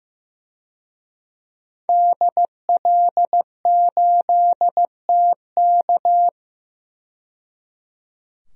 Morse code, heard as a keyed tone, about 16 wpm, 700 Hz. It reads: DL8TK
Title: DXCC-100 Call Signs - 5 Characters - 15wpm